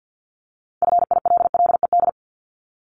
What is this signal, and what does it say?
Morse code, heard as a keyed tone, about 50 wpm, 700 Hz. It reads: FILLED
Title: Multi-letter Phonemes 50wpm